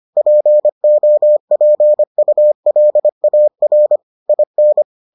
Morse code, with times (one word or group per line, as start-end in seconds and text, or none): 0.2-4.0 POPULAR
4.3-4.8 IN